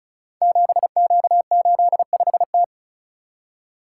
Morse code, heard as a keyed tone, about 36 wpm, 700 Hz. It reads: 7Q85T
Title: Code Groups - 5 Letter:Numbers - 35wpm